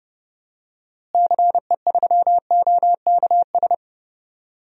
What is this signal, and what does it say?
Morse code, heard as a keyed tone, about 31 wpm, 700 Hz. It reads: CE3OKS
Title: DXCC-100 Call Signs - 6 Characters - Repeated 2x - 30wpm